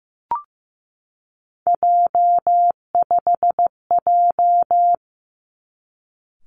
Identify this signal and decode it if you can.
Morse code, heard as a keyed tone, about 15 wpm, 700 Hz. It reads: J5J
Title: Three Challenging Letter-Number Combinations 15wpm